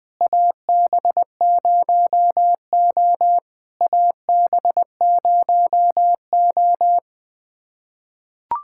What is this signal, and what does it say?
Morse code, heard as a keyed tone, about 20 wpm, 700 Hz. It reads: AB0O AB0O
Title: POTA QSO Practice - 20wpm